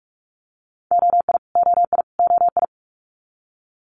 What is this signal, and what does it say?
Morse code, heard as a keyed tone, about 46 wpm, 700 Hz. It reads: OI OI OI
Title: Multi-letter Phonemes Repeated 3x 45wpm